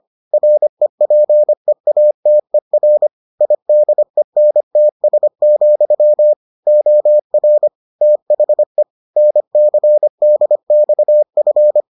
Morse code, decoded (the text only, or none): REPEATER IDENTS, OR THE NCDXF